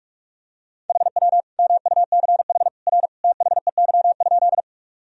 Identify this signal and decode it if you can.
Morse code, heard as a keyed tone, about 41 wpm, 700 Hz. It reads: HW MUCH R THEY?
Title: Sentences from Top 200 Words and Abbreviations 45wpm